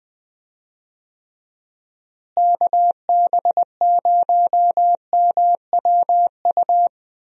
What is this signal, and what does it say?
Morse code, heard as a keyed tone, about 20 wpm, 700 Hz. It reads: KB0MWU
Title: USA Call Signs - 6 Characters - Repeated 3x - 20wpm